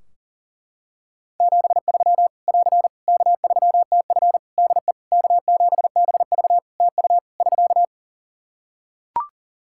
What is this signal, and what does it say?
Morse code, heard as a keyed tone, about 37 wpm, 700 Hz. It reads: 73 <AR> K3TF DE K7BV TU <SK>
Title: Practice QSO - Exchange 4 - Ending - 40wpm